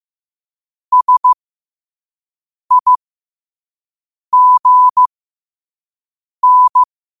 Morse code, heard as a keyed tone, about 15 wpm, spaced as for 5 wpm, 1000 Hz. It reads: SIGN
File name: qotdfiles / 5wpm